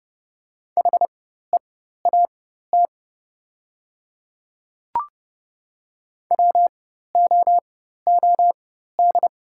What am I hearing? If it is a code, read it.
Morse code, heard as a keyed tone, about 29 wpm, spaced as for 14 wpm, 690 Hz. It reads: HEAT WOOD
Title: 4x Character Spacing - Top 500 Words - 30wpm